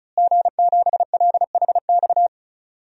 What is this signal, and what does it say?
Morse code, heard as a keyed tone, about 34 wpm, 700 Hz. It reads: G7LHX
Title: DXCC-100 Call Signs - 5 Characters - Repeated 3x - 35wpm